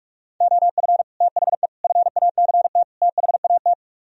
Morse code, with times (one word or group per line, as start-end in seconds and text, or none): 0.4-1.0 OF
1.2-1.7 THE
1.8-2.8 FACT
3.0-3.7 THAT